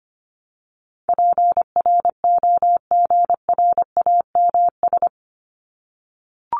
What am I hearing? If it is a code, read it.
Morse code, heard as a keyed tone, about 25 wpm, 700 Hz. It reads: PROGRAMS
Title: ES:S Suffix - Top 100 Words with Base Words 25wpm